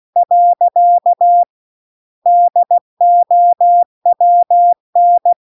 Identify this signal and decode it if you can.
Morse code, heard as a keyed tone, about 16 wpm, 700 Hz. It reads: . DOWN
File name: CW-The-Big-Twoheaded-River-2-16WPM